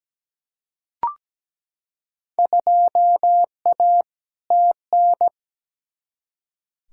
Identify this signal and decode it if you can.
Morse code, heard as a keyed tone, about 17 wpm, 700 Hz. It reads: 2A TN
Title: Field Day Exchanges - 17wpm